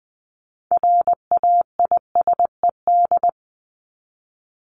Morse code, heard as a keyed tone, about 20 wpm, 700 Hz. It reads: RAISED
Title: ED Suffix - Top 100 Words with Base Words 20wpm